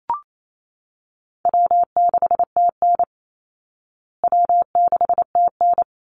Morse code, heard as a keyed tone, about 28 wpm, 700 Hz. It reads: W6TN W6TN
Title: Easy Call Signs Repeated 2x - 28wpm